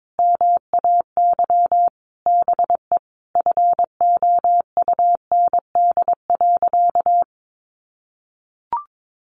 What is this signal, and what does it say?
Morse code, heard as a keyed tone, about 22 wpm, 700 Hz. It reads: MAY BE FOUND.
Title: Sentences from Top 1500 Words 22wpm